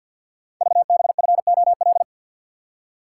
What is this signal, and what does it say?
Morse code, heard as a keyed tone, about 52 wpm, 700 Hz. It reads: VBFOL
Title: Code Groups - 5 Letters - 50wpm